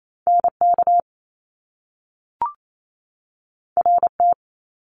Morse code, heard as a keyed tone, about 28 wpm, 700 Hz. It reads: NK RT
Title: Top Two Letter Combinations in Words 28wpm